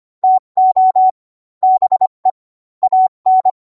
Morse code, heard as a keyed tone, about 25 wpm, 750 Hz. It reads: TO BE AN